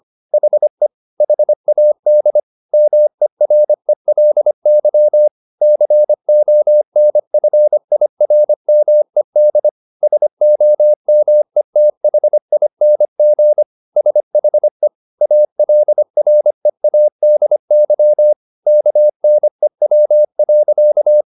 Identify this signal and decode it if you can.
Morse code, heard as a keyed tone, about 25 wpm, 600 Hz. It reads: HE HAD MERELY CONFIRMED SOMETHING SHE ALREADY KNEW.